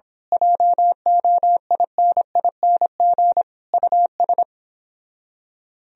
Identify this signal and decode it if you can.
Morse code, heard as a keyed tone, about 26 wpm, 700 Hz. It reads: JOINING US
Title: Speed-Racing - Sets of 5 Words 22wpm